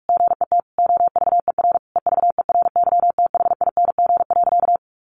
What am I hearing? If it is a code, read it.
Morse code, heard as a keyed tone, about 44 wpm, 700 Hz. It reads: GET OVER EVERYTHING.
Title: Sentences from Top 900 Words 45wpm